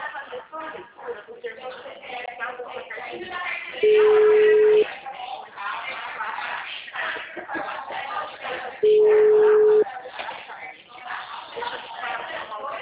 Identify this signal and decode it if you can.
Ringback tone (Europe, 425 Hz), with speech.
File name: out- -2001- - - .18745